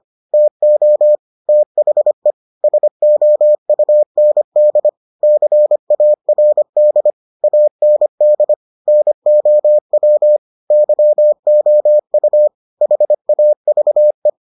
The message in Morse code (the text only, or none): TO THE SOUND CARD AND NOW YOU HAVE